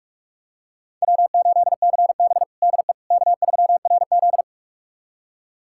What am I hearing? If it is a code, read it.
Morse code, heard as a keyed tone, about 46 wpm, 700 Hz. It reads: W8CB DE K3RZ
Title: USA Callsign DE Callsign - 4 Characters - 45wpm